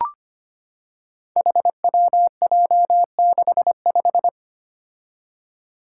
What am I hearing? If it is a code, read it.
Morse code, heard as a keyed tone, about 25 wpm, 700 Hz. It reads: HWJ65
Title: Five Challenging Letter-Number Combinations 25wpm